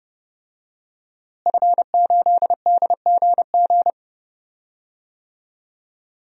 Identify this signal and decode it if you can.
Morse code, heard as a keyed tone, about 30 wpm, 700 Hz. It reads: F8DGG
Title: DXCC-100 Call Signs - 5 Characters - Repeated 3x - 30wpm